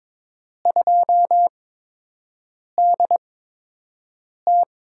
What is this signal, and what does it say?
Morse code, heard as a keyed tone, about 22 wpm, spaced as for 6 wpm, 700 Hz. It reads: 2DT
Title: USA Call Signs - 4 Characters - 8x Character Spacing - 22wpm